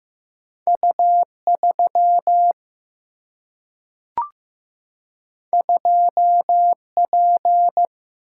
Morse code, heard as a keyed tone, about 15 wpm, 700 Hz. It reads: U 3 2 P
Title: Single-Letter Single-Number 15wpm